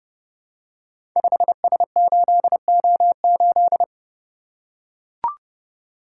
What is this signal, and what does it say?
Morse code, heard as a keyed tone, about 30 wpm, 700 Hz. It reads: 5S8O8
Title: Code Groups - 5 Letter:Numbers - 30wpm